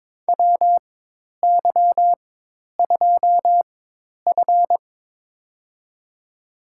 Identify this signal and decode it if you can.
Morse code, heard as a keyed tone, about 22 wpm, spaced as for 10 wpm, 700 Hz. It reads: WY2F
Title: USA Call Signs - 4x Character Spacing - 22wpm